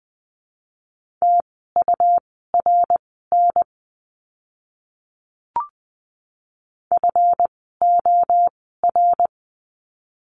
Morse code, heard as a keyed tone, about 20 wpm, spaced as for 14 wpm, 700 Hz. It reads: TURN FOR
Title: 192 Lesson 31f - Letter v - Words 20wpm